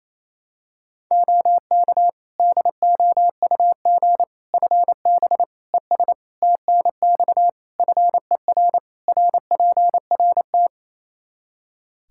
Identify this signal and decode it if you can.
Morse code, heard as a keyed tone, about 28 wpm, 700 Hz. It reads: OK DOUG FB ES TNX FER RPRT
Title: Practice QSO - Exchange 2 - Info - 28wpm